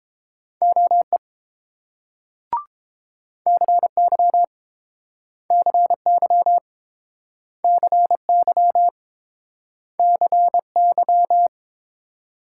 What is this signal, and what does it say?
Morse code, heard as a keyed tone, about 30 wpm, 700 Hz. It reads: OE CY CY CY CY